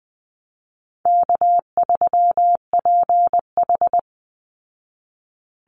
Morse code, heard as a keyed tone, about 20 wpm, 700 Hz. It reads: K3PH
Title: Call Signs Repeated 3x - 20wpm